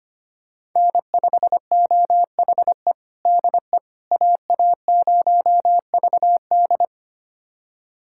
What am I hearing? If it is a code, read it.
Morse code, heard as a keyed tone, about 25 wpm, 700 Hz. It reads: N5OHE DE AA0VD
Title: USA Callsign DE Callsign - 5 Characters - 25wpm